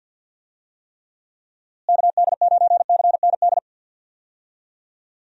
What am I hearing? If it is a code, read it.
Morse code, heard as a keyed tone, about 51 wpm, 700 Hz. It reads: KD9CND
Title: USA Call Signs - 6 Characters - Repeated 2x - 50wpm